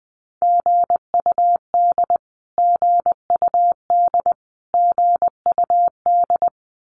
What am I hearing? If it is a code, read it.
Morse code, heard as a keyed tone, about 20 wpm, 700 Hz. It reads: GUD GUD GUD
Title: Abbreviations Repeated 3x - 20wpm